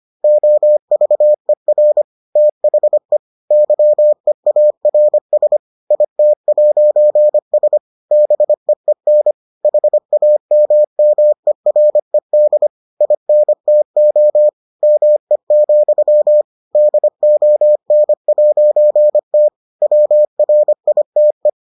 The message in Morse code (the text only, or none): OVER THE YEARS IT'S BEEN HAMMERED INTO ME, DON'T WRITE